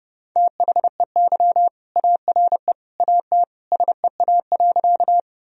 Morse code, heard as a keyed tone, about 32 wpm, 700 Hz. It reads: THEY ARE AT SEA.
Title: Sentences from Top 1000 Words - Review Part 1 - 30wpm